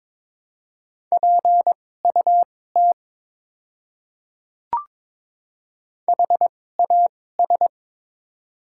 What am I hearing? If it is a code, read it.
Morse code, heard as a keyed tone, about 22 wpm, spaced as for 16 wpm, 700 Hz. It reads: PUT HAS
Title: 2x Character Spacing - Top 500 Words - 22wpm